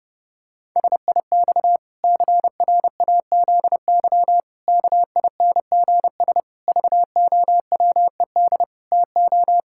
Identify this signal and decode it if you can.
Morse code, heard as a keyed tone, about 30 wpm, 700 Hz. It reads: SIX CRAZY KINGS VOWED TO
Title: Pangram Challenge v2 - 30wpm